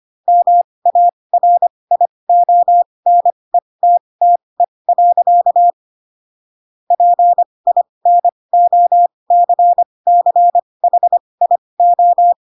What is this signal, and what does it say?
Morse code, heard as a keyed tone, about 25 wpm, spaced as for 20 wpm, 700 Hz. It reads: MARIONETTE. PINOCCHIO
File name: Pinocchio / Pinocchio ch8 15 25 ik1hge